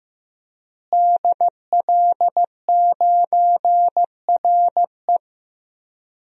Morse code, heard as a keyed tone, about 15 wpm, 700 Hz. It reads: DL9RE
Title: DXCC-100 Call Signs - 5 Characters - Repeated 2x - 15wpm